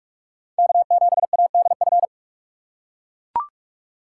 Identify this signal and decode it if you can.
Morse code, heard as a keyed tone, about 42 wpm, 700 Hz. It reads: K7ADF
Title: USA Call Signs - 6x Character Spacing - 45wpm